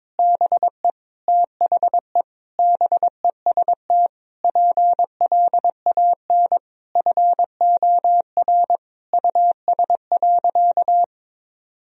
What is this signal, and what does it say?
Morse code, heard as a keyed tone, about 22 wpm, 700 Hz. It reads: BE THE BEST PLAN FOR US.
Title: Sentences from Top 1000 Words - Review Part 2 - 22wpm